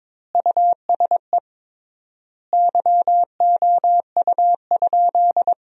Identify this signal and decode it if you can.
Morse code, heard as a keyed tone, about 23 wpm, 700 Hz. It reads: USE YOU?
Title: 3x Word-Spacing - Sentences from Top 200 Words - 22wpm